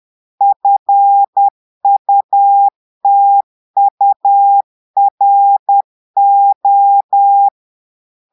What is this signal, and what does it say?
Morse code, heard as a keyed tone, about 10 wpm, 800 Hz. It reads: FUTURO
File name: Portugais - texte en clair 1 - vitesse - 10 mots minute